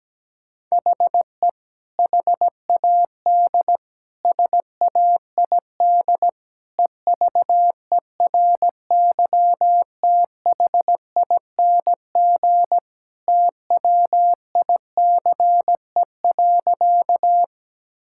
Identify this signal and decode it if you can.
Morse code, heard as a keyed tone, about 17 wpm, 700 Hz. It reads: HE HAD SAID EVERYTHING TWICE.
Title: Sentences from Top 1600 Words 17wpm